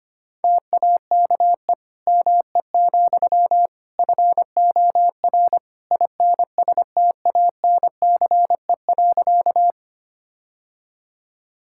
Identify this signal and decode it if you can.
Morse code, heard as a keyed tone, about 25 wpm, 700 Hz. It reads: TAKE ME, FOR INSTANCE.
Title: Sentences from Top 1000 Words - Review Part 5 - 25wpm